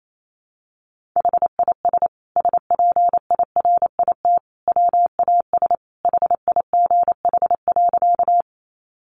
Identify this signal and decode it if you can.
Morse code, heard as a keyed tone, about 28 wpm, 700 Hz. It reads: HIS SPIRIT WAS HIGH.
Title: Sentences from Top 1400 Words 28wpm